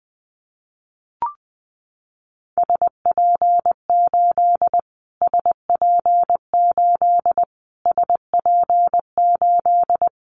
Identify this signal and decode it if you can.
Morse code, heard as a keyed tone, about 20 wpm, 700 Hz. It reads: SP8 SP8 SP8
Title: Callsign Prefixes for Top 25 DXCC - Repeated 3x - 20wpm